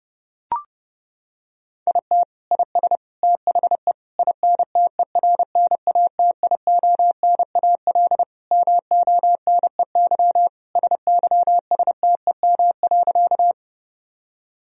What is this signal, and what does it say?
Morse code, heard as a keyed tone, about 30 wpm, 700 Hz. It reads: IT IS THE INTERNATIONAL MONEY SYSTEM.
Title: Sentences from Top 1000 Words - Review Part 4 - 30wpm